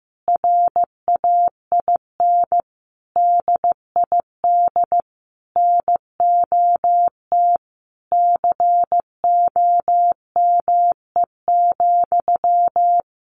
Morse code, heard as a keyed tone, about 15 wpm, 700 Hz. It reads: RAIN DID NOT COME,
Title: Sentences from Top 1300 Words 15wpm